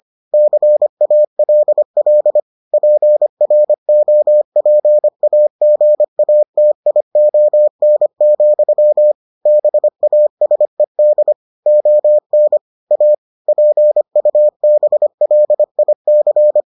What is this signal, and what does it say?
Morse code, heard as a keyed tone, about 25 wpm, 600 Hz. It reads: CALL PROPAGATION, BASED ON A PUBLIC